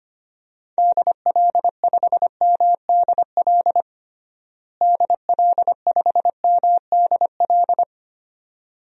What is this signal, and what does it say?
Morse code, heard as a keyed tone, about 25 wpm, 700 Hz. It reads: DL5MDL DL5MDL
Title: DXCC-100 Call Signs - 6 Characters - Repeated 2x - 25wpm